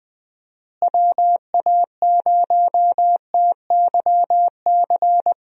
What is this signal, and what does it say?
Morse code, heard as a keyed tone, about 20 wpm, 700 Hz. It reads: WA0TYC
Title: USA Call Signs - 6 Characters - Repeated 3x - 20wpm